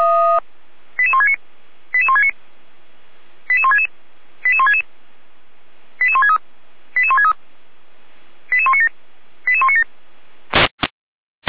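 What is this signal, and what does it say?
Five-tone selective call (ZVEI), heard as quick runs of five tones, tones of about 70 ms. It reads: 88169 88169 88166 88166 88163 88163 88197 88197
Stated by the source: ZVEI five-tone selective call 88447 received on SDR by fire brigade Feuerwehr Orlamünde.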